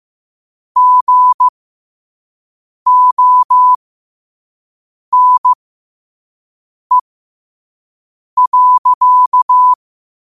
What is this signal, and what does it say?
Morse code, heard as a keyed tone, about 15 wpm, spaced as for 5 wpm, 1000 Hz. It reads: GONE.
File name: qotdfiles / 5wpm